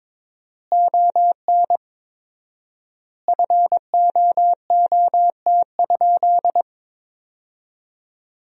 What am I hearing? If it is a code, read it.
Morse code, heard as a keyed tone, about 21 wpm, 700 Hz. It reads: ON FOOT?
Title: 4x Word-Spacing - Sentences from Top 500 Words - 22wpm